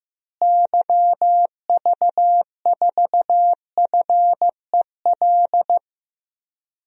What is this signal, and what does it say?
Morse code, heard as a keyed tone, about 15 wpm, 700 Hz. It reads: YV4FEL
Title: DXCC-100 Call Signs - 6 Characters - Repeated 3x - 15wpm